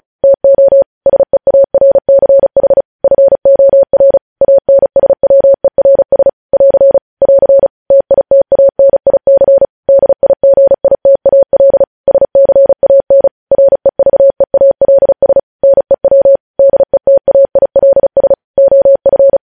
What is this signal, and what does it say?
Morse code, heard as a keyed tone, about 36 wpm, 550 Hz. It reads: TO SEARCH FOR ANSWERS <AR> <AR> TITANIC DIGITAL SCAN REVEALS NEW DETAILS OF
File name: podcast / News headlines-1x-35x35